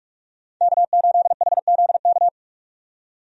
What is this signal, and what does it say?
Morse code, heard as a keyed tone, about 42 wpm, 700 Hz. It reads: K8HZK
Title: USA Call Signs - 5 Characters - 45wpm